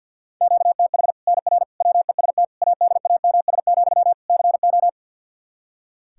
Morse code, heard as a keyed tone, about 46 wpm, 700 Hz. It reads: QTH NR WEST ADAMS, CO
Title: Practice QSO - Exchange 1 - Beginning - 50wpm